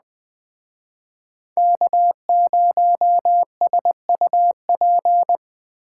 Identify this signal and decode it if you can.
Morse code, heard as a keyed tone, about 20 wpm, 700 Hz. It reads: K0SUP
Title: Speed Racing - USA 1x3 Call Signs 17wpm